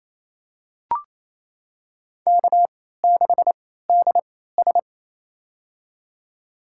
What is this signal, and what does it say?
Morse code, heard as a keyed tone, about 28 wpm, spaced as for 16 wpm, 700 Hz. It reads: K6DS
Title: USA Call Signs - 4 Characters - 3x Character Spacing - 28wpm